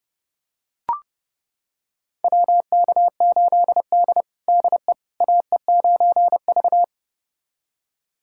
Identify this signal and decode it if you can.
Morse code, heard as a keyed tone, about 29 wpm, 700 Hz. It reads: WK8D DE AE9V
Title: USA Callsign DE Callsign - 4 Characters - 30wpm